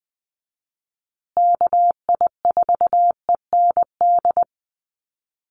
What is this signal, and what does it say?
Morse code, heard as a keyed tone, about 20 wpm, 700 Hz. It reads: KI4END
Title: USA Call Signs Repeated 3x 20wpm